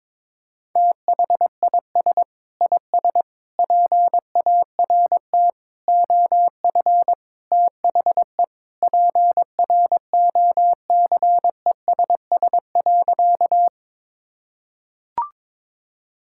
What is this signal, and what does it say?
Morse code, heard as a keyed tone, about 22 wpm, 700 Hz. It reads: THIS IS PART OF THE PROCESS.
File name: practice / 22wpm / Sentences with 2-Word Phrases - 22wpm